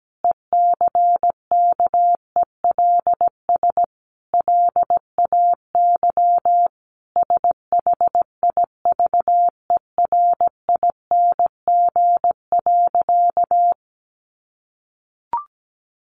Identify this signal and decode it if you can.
Morse code, heard as a keyed tone, about 17 wpm, 700 Hz. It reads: ECKELS LAY SHIVERING.
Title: A Sound Of Thunder 17wpm